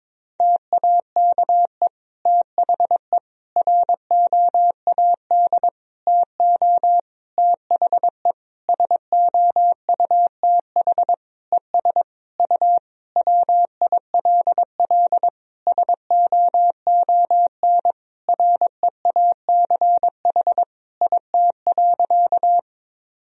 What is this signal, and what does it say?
Morse code, heard as a keyed tone, about 22 wpm, 700 Hz. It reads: TAKE THE ROAD TO THE SOUTH ES U WILL SOON REACH IT.
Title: Sentences from Top 400 Words and Abbreviations 22wpm